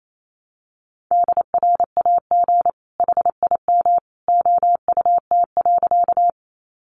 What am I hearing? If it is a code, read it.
Morse code, heard as a keyed tone, about 28 wpm, 700 Hz. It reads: DRAG HIM OUT.
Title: Sentences from Top 1500 Words 28wpm